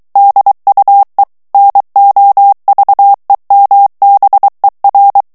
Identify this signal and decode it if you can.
Morse code, heard as a keyed tone, about 23 wpm, 800 Hz. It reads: DUE NOVEMBER
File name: ANGLAIS texte clair 3 vitesse 22 mots minute